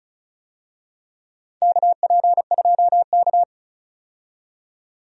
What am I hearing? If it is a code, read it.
Morse code, heard as a keyed tone, about 36 wpm, 700 Hz. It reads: KP2K
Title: DXCC-100 Call Signs - 4 Characters - 35wpm